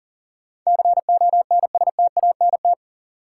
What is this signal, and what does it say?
Morse code, heard as a keyed tone, about 40 wpm, 700 Hz. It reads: CONSTANT
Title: Sentences from Top 1500 Words - New Word Drill - 40wpm